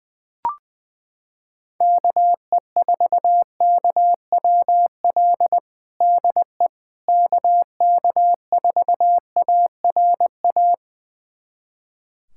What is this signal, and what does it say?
Morse code, heard as a keyed tone, about 20 wpm, 700 Hz. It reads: KE4KWL DE KK4ARA
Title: USA Callsign DE Callsign - 6 Characters - 20wpm